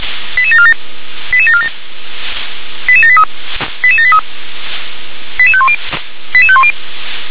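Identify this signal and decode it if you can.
Five-tone selective call (ZVEI), heard as quick runs of five tones, tones of about 70 ms. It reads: 88747 88747 88783 88783 88510 88510
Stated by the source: ZVEI five-tone selective call received on an SDR.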